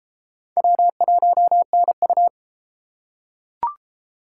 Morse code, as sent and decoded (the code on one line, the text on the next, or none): .-- .---- -. ..-
W1NU